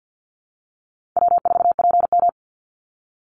Morse code, heard as a keyed tone, about 50 wpm, 700 Hz. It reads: W4PM
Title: Call Signs - 50wpm